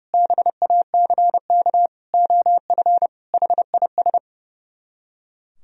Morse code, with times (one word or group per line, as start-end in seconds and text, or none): 0.1-1.9 BACK
2.1-3.1 OF
3.3-4.2 HIS